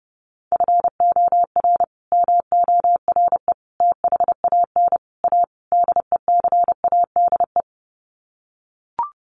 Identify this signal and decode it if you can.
Morse code, heard as a keyed tone, about 30 wpm, 700 Hz. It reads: FOR MORE THAN A DECADE
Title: Sets of 5 Words 30wpm